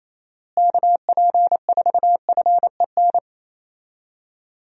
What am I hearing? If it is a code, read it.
Morse code, heard as a keyed tone, about 28 wpm, 700 Hz. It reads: KP4FEN
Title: DXCC-100 Call Signs - 6 Characters - Repeated 2x - 28wpm